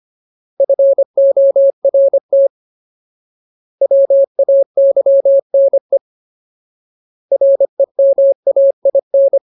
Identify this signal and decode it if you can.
Morse code, heard as a keyed tone, about 25 wpm, 550 Hz. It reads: FORT WAYNE REMAIN
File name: Ultimate Word List / 25WPM Ultimate